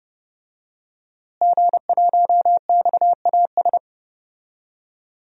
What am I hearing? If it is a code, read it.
Morse code, heard as a keyed tone, about 30 wpm, 700 Hz. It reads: G1XAS
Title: DXCC-100 Call Signs - 5 Characters - Repeated 3x - 30wpm